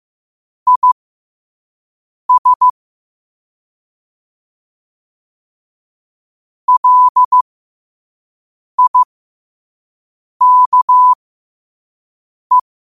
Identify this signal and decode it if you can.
Morse code, heard as a keyed tone, about 15 wpm, spaced as for 5 wpm, 1000 Hz. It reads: IS LIKE
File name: qotdfiles / 5wpm